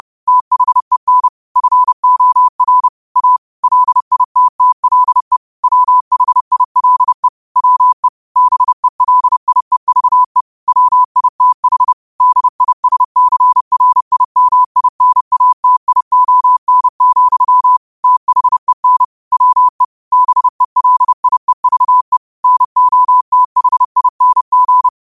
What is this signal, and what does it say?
Morse code, heard as a keyed tone, about 30 wpm, 1000 Hz. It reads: THEN FOR A LITTLE WHILE WE BELIEVE WITH DISCRIMINATION, THEN WE BELIEVE NOTHING
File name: qotdfiles / 30wpm